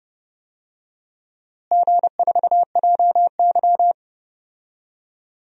Morse code, heard as a keyed tone, about 30 wpm, 700 Hz. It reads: G4JY
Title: DXCC-100 Call Signs - 4 Characters - Repeated 3x - 30wpm